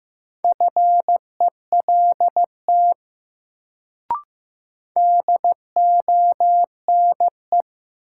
Morse code, heard as a keyed tone, about 15 wpm, 700 Hz. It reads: FELT DONE
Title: Sets of 5 Words Encore - New Word Drill - 15wpm